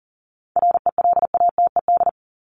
Morse code, heard as a keyed tone, about 40 wpm, 700 Hz. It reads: RELATED
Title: ED Suffix - Top 100 Words with Base Words 40wpm